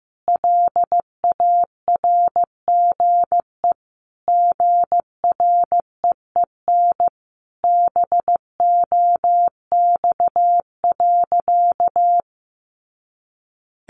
Morse code, heard as a keyed tone, about 15 wpm, 700 Hz. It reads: LARGE GREEN BOX.
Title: Sentences from Top 500 Words 15wpm